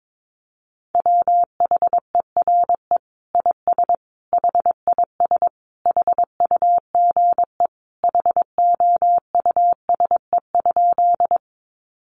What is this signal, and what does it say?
Morse code, heard as a keyed tone, about 22 wpm, 700 Hz. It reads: WHERE IS HIS HUGE HOUSE?
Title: Sentences from Top 1100 Words 22wpm